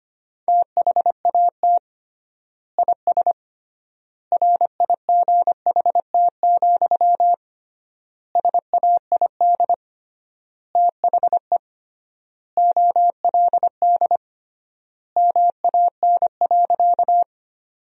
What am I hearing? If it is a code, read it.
Morse code, heard as a keyed tone, about 25 wpm, 700 Hz. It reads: THAT IS RIGHT, SAID THE OLD MAN.
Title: 3x Word-Spacing - Sentences from Top 200 Words - 25wpm